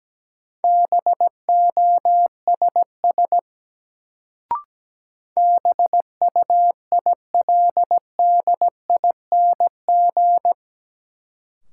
Morse code, heard as a keyed tone, about 17 wpm, 700 Hz. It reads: BOSS BUILDING
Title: Sentences from Top 700 Words - New Word Drill - 17wpm